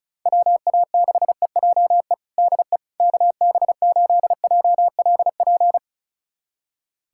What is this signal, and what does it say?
Morse code, heard as a keyed tone, about 36 wpm, 700 Hz. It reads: WA6EJE DE KB8JLP
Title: USA Callsign DE Callsign - 6 Characters - 35wpm